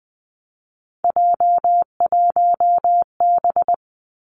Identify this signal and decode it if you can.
Morse code, heard as a keyed tone, about 20 wpm, 700 Hz. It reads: J1B
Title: Three Challenging Letter-Number Combinations 20wpm